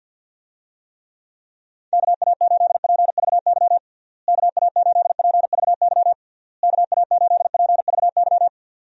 Morse code, heard as a keyed tone, about 50 wpm, 700 Hz. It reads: KA8PVY KA8PVY KA8PVY
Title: USA Call Signs - 6 Characters - Repeated 3x - 50wpm